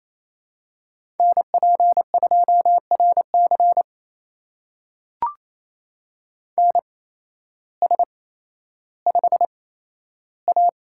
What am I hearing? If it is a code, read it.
Morse code, heard as a keyed tone, about 28 wpm, spaced as for 13 wpm, 700 Hz. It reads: NP2RC NS5A
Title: USA Call Signs - 8x Character Spacing - 28wpm